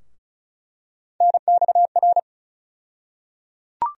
Morse code, heard as a keyed tone, about 34 wpm, 700 Hz. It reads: NXR
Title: USA Call Sign Postfixes - 3 Letters 35wpm